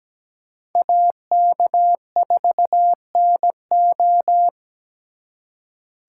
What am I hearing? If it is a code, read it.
Morse code, heard as a keyed tone, about 20 wpm, 700 Hz. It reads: AK4NO
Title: Speed Racing - USA 2x2 Call Signs 17wpm